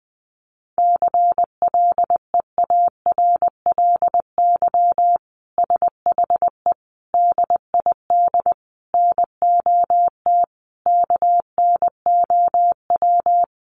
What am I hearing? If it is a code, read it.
Morse code, heard as a keyed tone, about 20 wpm, 700 Hz. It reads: CLEARLY SHE DID NOT KNOW
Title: Sentences from Top 1300 Words 20wpm